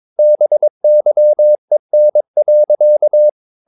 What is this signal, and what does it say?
Morse code, heard as a keyed tone, about 22 wpm, 600 Hz. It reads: BYEN.
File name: Ferie QRS 22wpm